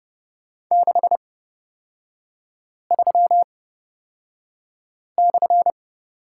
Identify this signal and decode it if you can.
Morse code, heard as a keyed tone, about 31 wpm, spaced as for 5 wpm, 700 Hz. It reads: 63/
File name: Lesson 36e - BT - Three Characters 30wpm